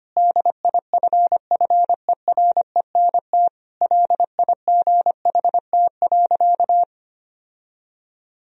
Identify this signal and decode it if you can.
Morse code, heard as a keyed tone, about 25 wpm, 700 Hz. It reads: DIFFERENT LIGHT.
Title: Sentences from Top 1000 Words - Review Part 3 - 25wpm